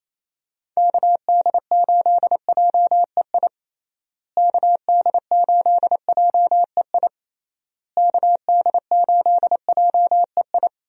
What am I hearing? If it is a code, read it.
Morse code, heard as a keyed tone, about 27 wpm, 700 Hz. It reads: KD8JEI KD8JEI KD8JEI
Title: USA Call Signs - 6 Characters - Repeated 3x - 28wpm